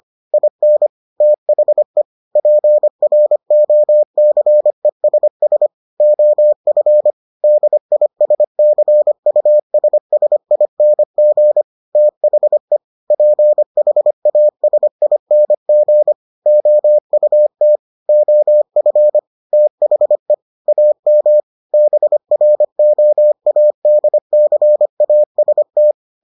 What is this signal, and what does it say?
Morse code, heard as a keyed tone, about 25 wpm, 600 Hz. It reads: IN THE PROCESS OF DISCUSSING THE PHASING OUT OF THE AM BROADCAST